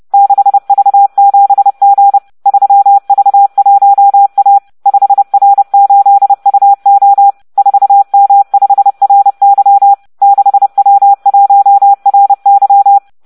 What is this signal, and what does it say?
Morse code, heard as a keyed tone, about 30 wpm, 800 Hz. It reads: 6V7G 3V1A 5R8UO 4M5RY 6W1RY